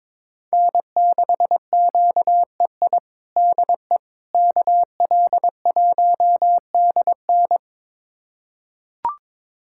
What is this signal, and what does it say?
Morse code, heard as a keyed tone, about 22 wpm, 700 Hz. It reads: N6QEI DE KL1DN
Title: USA Callsign DE Callsign - 5 Characters - 22wpm